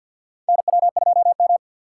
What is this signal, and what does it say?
Morse code, heard as a keyed tone, about 48 wpm, 700 Hz. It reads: NW2M
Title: USA Call Signs - 6x Character Spacing - 50wpm